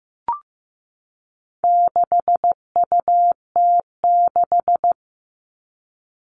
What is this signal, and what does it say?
Morse code, heard as a keyed tone, about 15 wpm, 700 Hz. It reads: 6UT6
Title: All Cut Numbers - 4-Digit 15wpm